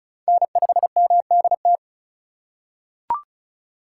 Morse code, heard as a keyed tone, about 34 wpm, 700 Hz. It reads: N5MDT
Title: USA Call Signs - 2x Character Spacing - 35wpm